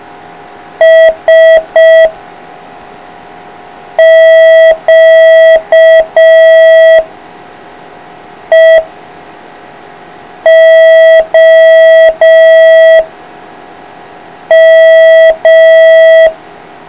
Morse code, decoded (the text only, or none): S Q E O M